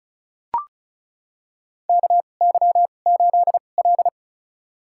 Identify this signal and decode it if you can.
Morse code, heard as a keyed tone, about 34 wpm, spaced as for 25 wpm, 700 Hz. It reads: KY8L
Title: USA Call Signs - 4 Characters - 2x Character Spacing - 35wpm